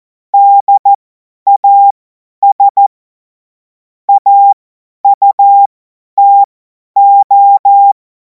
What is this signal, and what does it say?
Morse code, heard as a keyed tone, about 14 wpm, spaced as for 10 wpm, 800 Hz. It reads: DAS AUTO
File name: Morsekurs / Witze / Witz 28. April 2026 50 F70